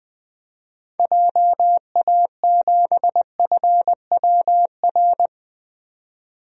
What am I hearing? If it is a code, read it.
Morse code, heard as a keyed tone, about 20 wpm, 700 Hz. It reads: JA7FWR
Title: DXCC-100 Call Signs - 6 Characters - 20wpm